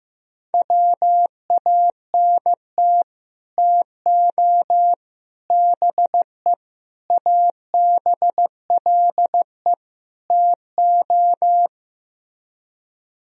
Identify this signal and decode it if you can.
Morse code, heard as a keyed tone, about 15 wpm, 700 Hz. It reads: WANT TO BE ABLE TO
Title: Speed-Racing - Sets of 5 Words 15wpm